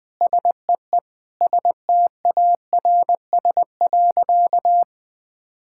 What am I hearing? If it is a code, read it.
Morse code, heard as a keyed tone, about 20 wpm, 700 Hz. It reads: SEE STARS.
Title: Sentences with 4-Word Phrases - 20wpm